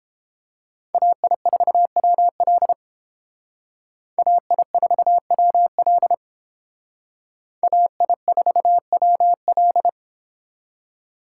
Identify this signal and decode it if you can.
Morse code, heard as a keyed tone, about 28 wpm, 700 Hz. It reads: AI4WL AI4WL AI4WL